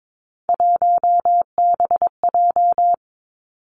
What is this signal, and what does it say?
Morse code, heard as a keyed tone, about 22 wpm, 700 Hz. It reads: 1BJ
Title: Three Challenging Letter-Number Combinations 22wpm